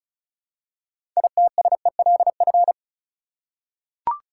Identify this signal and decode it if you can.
Morse code, heard as a keyed tone, about 34 wpm, 700 Hz. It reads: ITSELF
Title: Six Letter Words - 35wpm